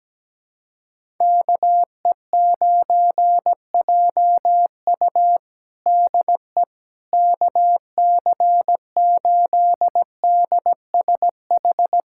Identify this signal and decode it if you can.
Morse code, heard as a keyed tone, about 17 wpm, 700 Hz. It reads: KE9JU DE KC8DSH
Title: USA Callsign DE Callsign 17wpm